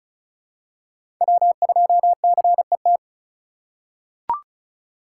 Morse code, heard as a keyed tone, about 34 wpm, 700 Hz. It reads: W2CET
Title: USA Call Signs - 5 Characters - 4x Character Spacing - 35wpm